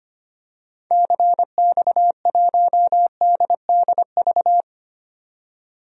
Morse code, heard as a keyed tone, about 25 wpm, 700 Hz. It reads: CX1DDV
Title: DXCC-100 Call Signs - 6 Characters - 25wpm